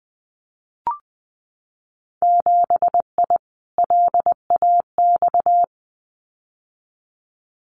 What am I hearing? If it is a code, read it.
Morse code, heard as a keyed tone, about 20 wpm, 700 Hz. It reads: 7I LAX
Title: Winter Field Day Exchanges Encore 20wpm